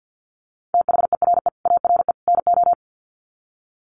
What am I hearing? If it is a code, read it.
Morse code, heard as a keyed tone, about 50 wpm, 700 Hz. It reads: THERE ARE NO
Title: Sets of 3 Words 50wpm